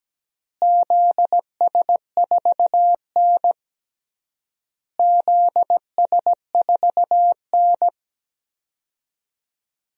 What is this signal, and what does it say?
Morse code, heard as a keyed tone, about 17 wpm, 700 Hz. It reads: ZS4N ZS4N
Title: DXCC-100 Call Signs - 4 Characters - Repeated 3x - 17wpm